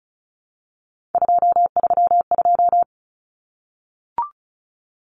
Morse code, heard as a keyed tone, about 34 wpm, 700 Hz. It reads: 232
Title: Common Cut Numbers - 3-Digit 35wpm